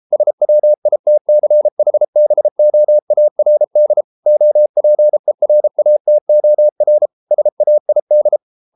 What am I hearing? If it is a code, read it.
Morse code, heard as a keyed tone, about 33 wpm, 600 Hz. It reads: SWITCHBOARD OPERATOR SAID